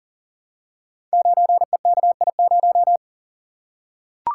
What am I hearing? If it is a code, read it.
Morse code, heard as a keyed tone, about 39 wpm, 700 Hz. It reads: 9EKI0
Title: Code Groups - 5 Letter:Numbers - 40wpm